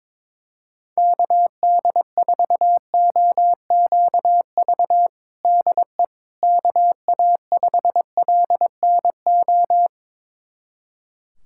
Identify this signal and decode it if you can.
Morse code, heard as a keyed tone, about 22 wpm, 700 Hz. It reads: KD4OQV DE KA5LNO
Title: USA Callsign DE Callsign - 6 Characters - 22wpm